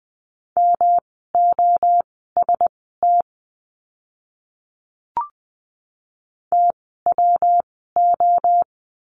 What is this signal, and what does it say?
Morse code, heard as a keyed tone, about 20 wpm, spaced as for 14 wpm, 700 Hz. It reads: MOST TWO
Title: 139 Lesson 23f - Letter f - Words 20wpm